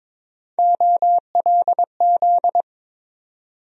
Morse code, heard as a keyed tone, about 22 wpm, 700 Hz. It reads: OLZ
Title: USA Call Sign Postfixes - 3 Letters 22wpm